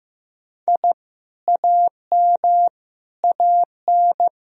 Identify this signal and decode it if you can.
Morse code, heard as a keyed tone, about 15 wpm, 700 Hz. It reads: I AM AN
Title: Sentences from Top 1000 Words - Review Part 1 - 15wpm